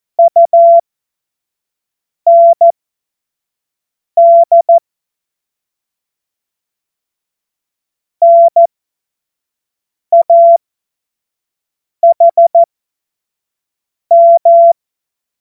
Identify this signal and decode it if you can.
Morse code, heard as a keyed tone, about 14 wpm, spaced as for 5 wpm, 680 Hz. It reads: UND NAHM
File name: Morsekurs / Witze / Witz 16. Dezember 2025 25 F70